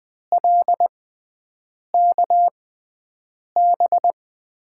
Morse code, heard as a keyed tone, about 20 wpm, spaced as for 7 wpm, 700 Hz. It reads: LKB
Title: USA Call Signs - 5 Characters - 6x Character Spacing - 20wpm